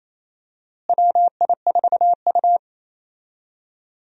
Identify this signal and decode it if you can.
Morse code, heard as a keyed tone, about 28 wpm, 700 Hz. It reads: WI4U
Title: USA Call Signs - 4 Characters - Repeated 3x - 28wpm